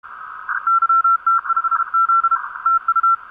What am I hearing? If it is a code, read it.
Morse code, heard as a keyed tone, about 32 wpm, 1350 Hz. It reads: YT5Z TU
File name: vk9dwx 30 cw